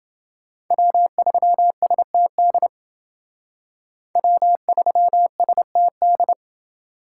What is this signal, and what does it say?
Morse code, heard as a keyed tone, about 29 wpm, 700 Hz. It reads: W3STD W3STD